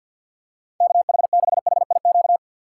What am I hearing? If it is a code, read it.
Morse code, heard as a keyed tone, about 54 wpm, 700 Hz. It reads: KH6HIQ
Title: USA Call Signs - 4x Character Spacing - 50wpm